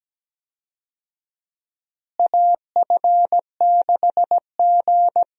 Morse code, heard as a keyed tone, about 17 wpm, 700 Hz. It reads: AF6G
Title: USA Call Signs - 4 Characters - Repeated 2x - 17wpm